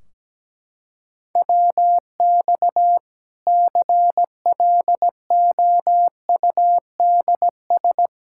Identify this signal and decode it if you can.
Morse code, heard as a keyed tone, about 17 wpm, 700 Hz. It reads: WX CLOUDS
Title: K1USN SST QSO Practice 17wpm